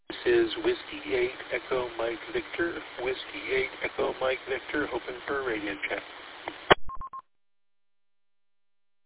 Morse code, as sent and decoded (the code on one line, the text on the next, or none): none